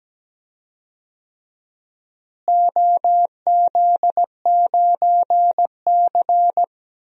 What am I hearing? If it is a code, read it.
Morse code, heard as a keyed tone, about 17 wpm, 700 Hz. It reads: OZ9C
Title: DXCC-100 Call Signs - 4 Characters - Repeated 2x - 17wpm